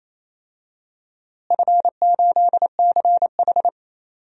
Morse code, heard as a keyed tone, about 28 wpm, 700 Hz. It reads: F8CH
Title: DXCC-100 Call Signs - 4 Characters - Repeated 3x - 28wpm